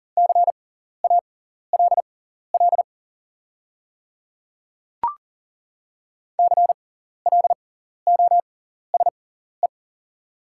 Morse code, heard as a keyed tone, about 40 wpm, spaced as for 14 wpm, 690 Hz. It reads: CALL CLOSE
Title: 6x Character Spacing - Top 300 Words - 40wpm